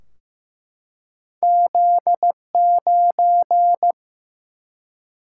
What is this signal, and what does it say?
Morse code, heard as a keyed tone, about 15 wpm, 700 Hz. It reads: Z9
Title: Single-Letter Single-Number 15wpm